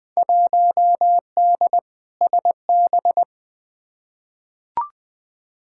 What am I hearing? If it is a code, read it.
Morse code, heard as a keyed tone, about 20 wpm, 700 Hz. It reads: 1D SB
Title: Field Day Exchanges - 20wpm